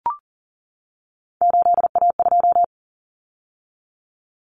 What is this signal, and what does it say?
Morse code, heard as a keyed tone, about 40 wpm, 700 Hz. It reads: 8A2
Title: Common Cut Numbers - 3-Digit 40wpm